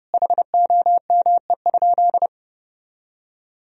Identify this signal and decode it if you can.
Morse code, heard as a keyed tone, about 32 wpm, 700 Hz. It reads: HOME?
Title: Sentences from Top 1000 Words - Review Part 3 - 30wpm